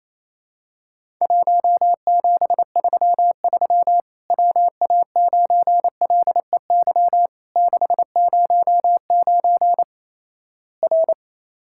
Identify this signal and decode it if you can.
Morse code, heard as a keyed tone, about 28 wpm, 700 Hz. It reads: 1733 WA9LEY 609 R
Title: Worked All Europe DX Contest - QTC Exchange - 28wpm